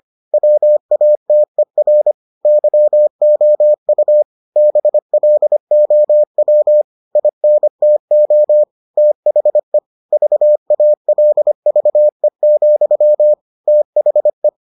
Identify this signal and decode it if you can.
Morse code, heard as a keyed tone, about 25 wpm, 600 Hz. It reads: WATER YOU BLOW INTO THE VALVE, THE